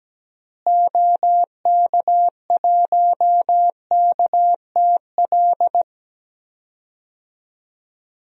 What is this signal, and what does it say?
Morse code, heard as a keyed tone, about 17 wpm, 700 Hz. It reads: OK1KTL
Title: DXCC-100 Call Signs - 6 Characters - Repeated 2x - 17wpm